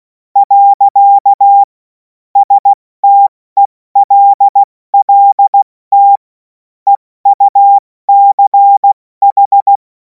Morse code, heard as a keyed tone, about 16 wpm, 800 Hz. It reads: . STELLT EUCH
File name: Morsekurs / Witze / Witz 11. August 2026 70 F80